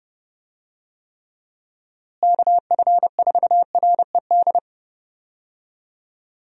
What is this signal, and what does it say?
Morse code, heard as a keyed tone, about 30 wpm, 700 Hz. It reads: KF4RED
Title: USA Call Signs - 6 Characters - Repeated 3x - 30wpm